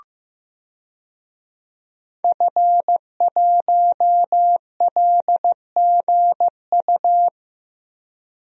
Morse code, heard as a keyed tone, about 15 wpm, 700 Hz. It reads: F1LGU
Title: DXCC-100 Call Signs - 5 Characters - Repeated 2x - 15wpm